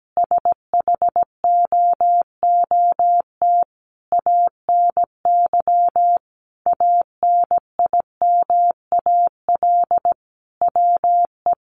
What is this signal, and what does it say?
Morse code, heard as a keyed tone, about 17 wpm, 700 Hz. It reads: SHOOT ANY ANIMAL WE
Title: A Sound Of Thunder 17wpm